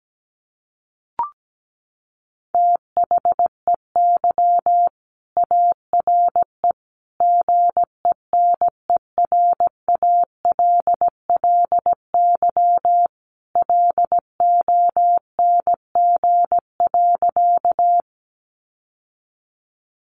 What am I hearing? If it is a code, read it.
Morse code, heard as a keyed tone, about 17 wpm, 700 Hz. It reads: THEY ARE GENERALLY LONG.
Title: Sentences from Top 600 Words 17wpm